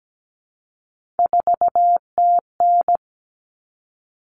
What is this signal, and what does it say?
Morse code, heard as a keyed tone, about 17 wpm, 700 Hz. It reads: 4TN
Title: All Cut Numbers - 3-Digit 17wpm